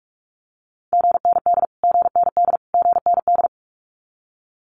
Morse code, heard as a keyed tone, about 44 wpm, 700 Hz. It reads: GND GND GND
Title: Abbreviations Repeated 3x - 45wpm